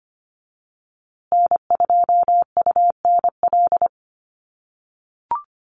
Morse code, heard as a keyed tone, about 25 wpm, 700 Hz. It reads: N2UNL
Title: USA Call Signs Repeated 2x 25wpm